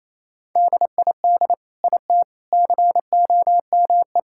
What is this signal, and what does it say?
Morse code, heard as a keyed tone, about 28 wpm, 700 Hz. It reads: DID IT COME
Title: Sentences from Top 1000 Words - Review Part 5 - 28wpm